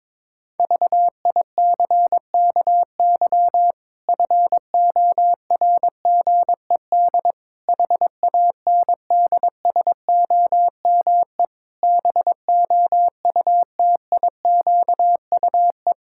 Morse code, heard as a keyed tone, about 21 wpm, 700 Hz. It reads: VICKY FORGED HANDSOME BOUTIQUE
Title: Pangram Challenge v2 - 22wpm